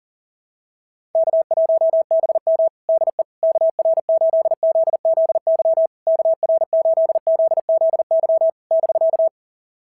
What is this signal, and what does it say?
Morse code, heard as a keyed tone, about 40 wpm, 690 Hz. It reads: K1BM DE KR8ZZY KR8ZZY <BK>
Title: SOTA QSO Practice 40wpm